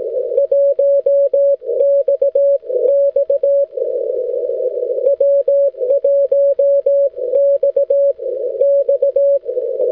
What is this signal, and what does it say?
Morse code, heard as a keyed tone, about 17 wpm, spaced as for 14 wpm, 560 Hz. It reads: 1 XX W1 XX
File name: wl7e-echo-160m